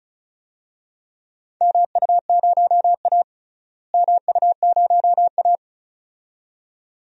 Morse code, heard as a keyed tone, about 36 wpm, 700 Hz. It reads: MU0A MU0A
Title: DXCC-100 Call Signs - 4 Characters - Repeated 2x - 35wpm